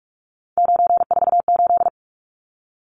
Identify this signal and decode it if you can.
Morse code, heard as a keyed tone, about 44 wpm, 700 Hz. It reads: 948
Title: Numbers - 3-Digit 45wpm